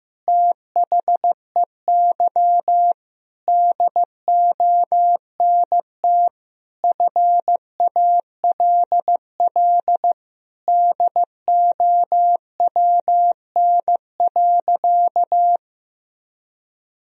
Morse code, heard as a keyed tone, about 16 wpm, 700 Hz. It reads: THEY DONT FALL DOWN.
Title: Sentences from Top 1000 Words - Review Part 2 - 15wpm